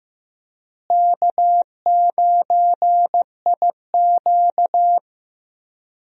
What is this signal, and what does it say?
Morse code, heard as a keyed tone, about 15 wpm, 700 Hz. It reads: K9IQ
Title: USA Call Signs - 4 Characters - Repeated 2x - 15wpm